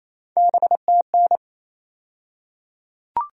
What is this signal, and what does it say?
Morse code, heard as a keyed tone, about 28 wpm, 700 Hz. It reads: BTN
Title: USA Call Sign Postfixes - 3 Letters 28wpm